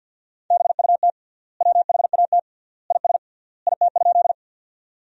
Morse code, heard as a keyed tone, about 50 wpm, 700 Hz. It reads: BUT WHAT IS IT?
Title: 3x Word-Spacing - Sentences from Top 100 Words - 50wpm